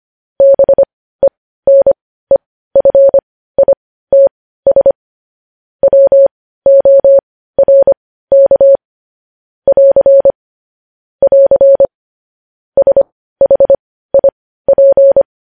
Morse code, heard as a keyed tone, about 25 wpm, spaced as for 15 wpm, 550 Hz. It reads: BENEFITS WORK <AR> <AR> SHIP
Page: news-headlines-in-morse-code-at-15-wpm